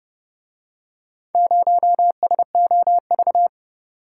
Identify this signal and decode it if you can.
Morse code, heard as a keyed tone, about 29 wpm, 700 Hz. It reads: 0SOV